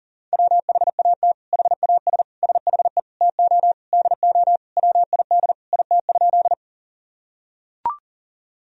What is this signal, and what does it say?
Morse code, heard as a keyed tone, about 37 wpm, 700 Hz. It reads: WHAT HAS SHE TO DO WID IT?
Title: Sentences from Top 100 Words and Abbreviations 40wpm